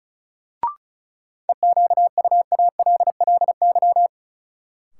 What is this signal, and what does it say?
Morse code, heard as a keyed tone, about 34 wpm, 700 Hz. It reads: EQUALLY
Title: Sentences from Top 1400 Words - New Word Drill - 35wpm